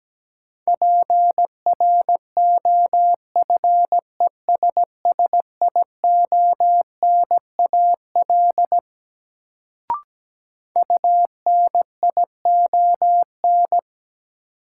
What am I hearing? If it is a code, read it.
Morse code, heard as a keyed tone, about 17 wpm, 700 Hz. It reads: PROFESSIONAL UNION
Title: Sentences from Top 1400 Words - New Word Drill Alt - 17wpm